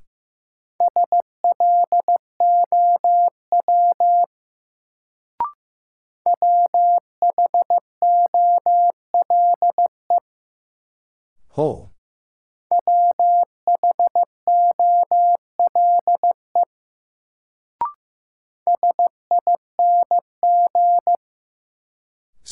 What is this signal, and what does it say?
Morse code, heard as a keyed tone, about 15 wpm, 700 Hz. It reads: SLOW WHOLE WHOLE SING
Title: Sentences from Top 400 Words - New Word Drill - 15wpm